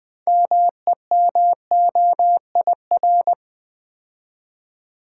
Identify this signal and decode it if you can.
Morse code, heard as a keyed tone, about 21 wpm, 700 Hz. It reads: MEMOIR
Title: 6 Letter Words - First 3 Letters Given - 20wpm